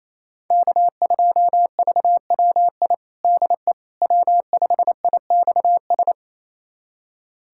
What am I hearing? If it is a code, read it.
Morse code, heard as a keyed tone, about 28 wpm, 700 Hz. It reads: K2VWI DE W5IXS
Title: USA Callsign DE Callsign - 5 Characters - 28wpm